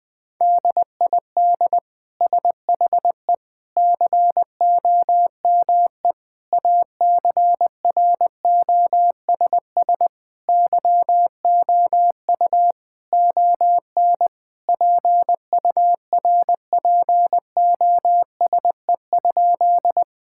Morse code, heard as a keyed tone, about 20 wpm, 700 Hz. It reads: DID SHE COME ACROSS YOU ON PURPOSE?
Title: Sentences from Top 1000 Words - Review Part 5 - 20wpm